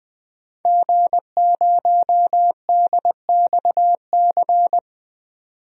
Morse code, heard as a keyed tone, about 20 wpm, 700 Hz. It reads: G0DXC
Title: DXCC-100 Call Signs - 5 Characters - 20wpm